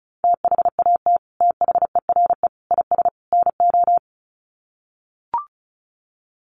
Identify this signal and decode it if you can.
Morse code, heard as a keyed tone, about 34 wpm, 700 Hz. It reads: THAT THERE IS NO
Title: Sets of 4 Words 35wpm